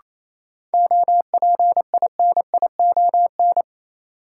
Morse code, heard as a keyed tone, about 28 wpm, 700 Hz. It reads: OPINION
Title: Sentences from Top 1200 Words - New Word Drill Alt - 28wpm